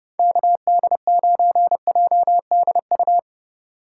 Morse code, heard as a keyed tone, about 30 wpm, 700 Hz. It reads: KD9JDU